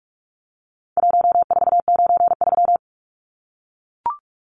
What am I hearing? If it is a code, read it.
Morse code, heard as a keyed tone, about 44 wpm, 700 Hz. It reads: 1493
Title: Numbers - Contest Style - 4-Digit 45wpm